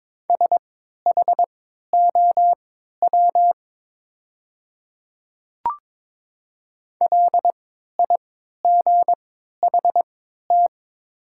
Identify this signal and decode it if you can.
Morse code, heard as a keyed tone, about 22 wpm, spaced as for 12 wpm, 700 Hz. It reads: SHOW LIGHT
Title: 3x Character Spacing - Top 300 Words - 22wpm